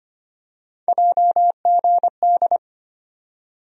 Morse code, heard as a keyed tone, about 25 wpm, 700 Hz. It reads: JGD
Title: USA Call Sign Postfixes - 3 Letters 25wpm